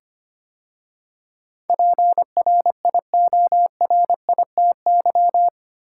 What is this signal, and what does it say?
Morse code, heard as a keyed tone, about 25 wpm, 700 Hz. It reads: PRIORITY
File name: 8 Letter Words - First 3 Letters Given - 25wpm